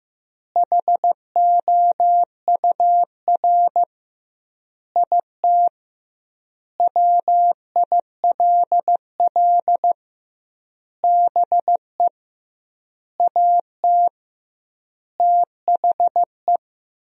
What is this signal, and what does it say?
Morse code, heard as a keyed tone, about 15 wpm, 700 Hz. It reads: HOUR IT WILL BE AT THE
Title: 2x Word-Spacing - Sentences from Top 400 Words - 15wpm